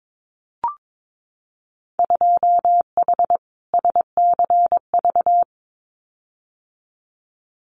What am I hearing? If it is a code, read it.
Morse code, heard as a keyed tone, about 22 wpm, 700 Hz. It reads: 2H SCV
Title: Winter Field Day Exchanges Encore 22wpm